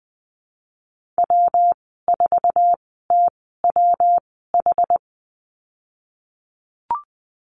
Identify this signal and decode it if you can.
Morse code, heard as a keyed tone, about 20 wpm, spaced as for 15 wpm, 700 Hz. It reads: W4TWH
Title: 186 Lesson 30g - Letter b - Call Signs 20wpm